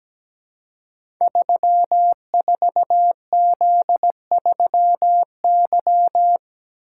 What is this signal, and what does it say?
Morse code, heard as a keyed tone, about 17 wpm, 700 Hz. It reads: 34Z3Y
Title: Code Groups - 5 Letter:Numbers - 17wpm